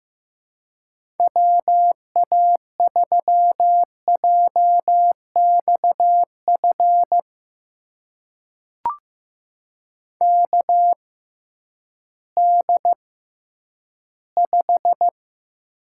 Morse code, heard as a keyed tone, about 15 wpm, spaced as for 10 wpm, 700 Hz. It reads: WA3JXF KD5
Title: USA Call Signs - 6 Characters - 6x Character Spacing - 15wpm